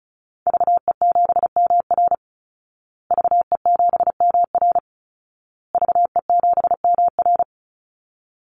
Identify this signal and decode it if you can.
Morse code, heard as a keyed tone, about 34 wpm, 700 Hz. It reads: VE7MR VE7MR VE7MR
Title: Call Signs Repeated 2x - 35wpm